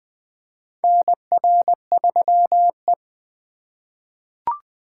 Morse code, heard as a keyed tone, about 20 wpm, 700 Hz. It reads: NR3E
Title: 2023 SOTA Chaser Call Signs - 20wpm